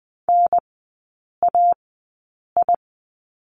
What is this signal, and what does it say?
Morse code, heard as a keyed tone, about 20 wpm, spaced as for 8 wpm, 700 Hz. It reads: NAI
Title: 035 Lesson 7e - Letter s - Three Characters 20wpm